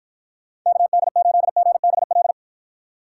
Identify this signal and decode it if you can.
Morse code, heard as a keyed tone, about 52 wpm, 700 Hz. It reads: KD8GBL